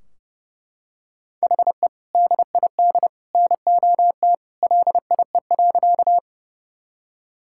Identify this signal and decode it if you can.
Morse code, heard as a keyed tone, about 32 wpm, 700 Hz. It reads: HE DID NOT LIE.
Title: Sentences from Top 1600 Words 30wpm